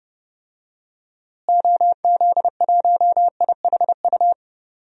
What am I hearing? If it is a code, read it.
Morse code, heard as a keyed tone, about 31 wpm, 700 Hz. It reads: OZ1IHU
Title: DXCC-100 Call Signs - 6 Characters - Repeated 2x - 30wpm